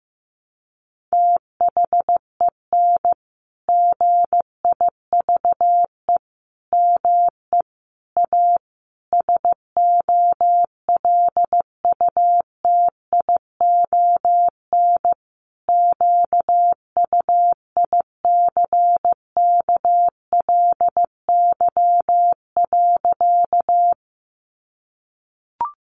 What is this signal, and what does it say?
Morse code, heard as a keyed tone, about 15 wpm, 700 Hz. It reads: THEN GIVE ME A SOLUTION QUICKLY.
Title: Sentences from Top 1300 Words 15wpm